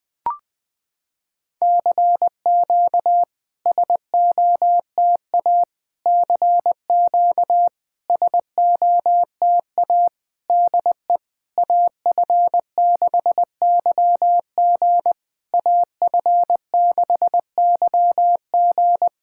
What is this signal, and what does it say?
Morse code, heard as a keyed tone, about 20 wpm, 690 Hz. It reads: CQ SOTA CQ SOTA DE AF6YG AF6YG
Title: SOTA QSO Practice 20wpm